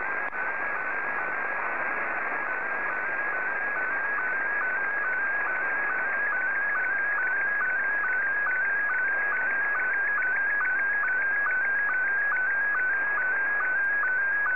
SSTV (slow-scan television), Scottie 1.
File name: sstv / S1-VK2OY